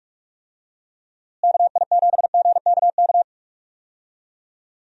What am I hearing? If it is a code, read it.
Morse code, heard as a keyed tone, about 43 wpm, 700 Hz. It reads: KI7GKK
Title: USA Call Signs - 6 Characters - Repeated 3x - 45wpm